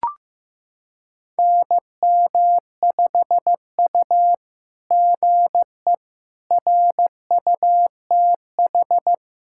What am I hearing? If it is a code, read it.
Morse code, heard as a keyed tone, about 15 wpm, 700 Hz. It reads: NM5U GE RUTH
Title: SOTA QSO Practice 15wpm